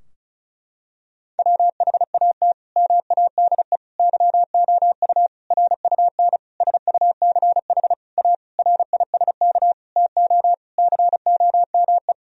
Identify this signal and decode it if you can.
Morse code, heard as a keyed tone, about 36 wpm, 700 Hz. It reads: WHAT MADE YOU RUN SUCH A RISK TO COME
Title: Sentences from Top 1000 Words - Review Part 4 - 35wpm